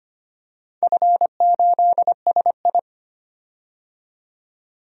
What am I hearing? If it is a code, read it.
Morse code, heard as a keyed tone, about 25 wpm, 700 Hz. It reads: F8SI
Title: DXCC-100 Call Signs - 4 Characters - 25wpm